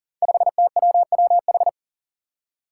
Morse code, heard as a keyed tone, about 43 wpm, 700 Hz. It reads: 5TWWH
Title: Five Challenging Letter-Number Combinations 40wpm